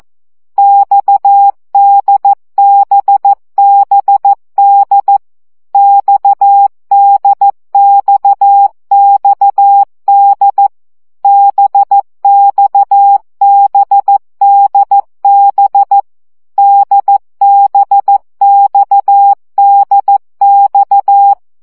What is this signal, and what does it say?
Morse code, heard as a keyed tone, about 14 wpm, 800 Hz. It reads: XDBBD XDXXD BXBDB DBXDX
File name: CW gyakorlas d b x 12 baud